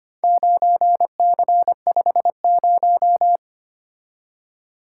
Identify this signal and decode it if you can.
Morse code, heard as a keyed tone, about 25 wpm, 700 Hz. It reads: 9C50
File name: practice / 25wpm / Four Letter-Number Combinations - 25wpm